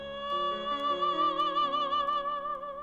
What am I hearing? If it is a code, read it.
There is no code.